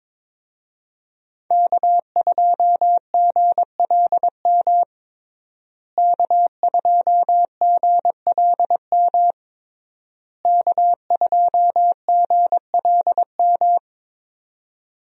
Morse code, heard as a keyed tone, about 22 wpm, 700 Hz. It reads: K2GLM K2GLM K2GLM
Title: USA Call Signs - 5 Characters - Repeated 3x - 22wpm